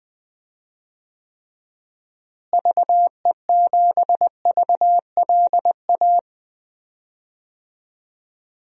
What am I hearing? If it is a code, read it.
Morse code, heard as a keyed tone, about 21 wpm, 700 Hz. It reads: VE7VLA
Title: DXCC-100 Call Signs - 6 Characters - Repeated 2x - 20wpm